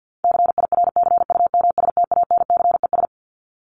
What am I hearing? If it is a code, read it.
Morse code, heard as a keyed tone, about 50 wpm, 700 Hz. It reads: CIRCUMSTANCES
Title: ES:S Suffix - Top 100 Words with Base Words 50wpm